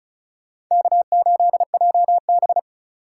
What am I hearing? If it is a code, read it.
Morse code, heard as a keyed tone, about 45 wpm, 700 Hz. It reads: K8JB